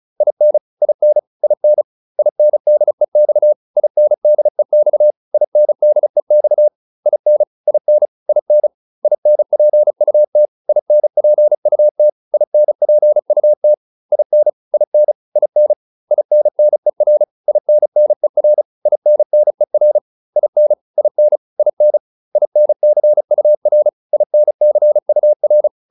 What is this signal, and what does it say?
Morse code, heard as a keyed tone, about 34 wpm, 600 Hz. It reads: IN IN IN INDEX INDEX INDEX IN IN IN INPUT INPUT INPUT IN IN IN INNER INNER INNER IN IN IN INCUR INCUR
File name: IN5 35